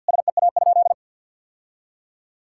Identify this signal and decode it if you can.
Morse code, heard as a keyed tone, about 52 wpm, 700 Hz. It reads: SEA?
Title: Sentences from Top 1000 Words - Review Part 6 - 50wpm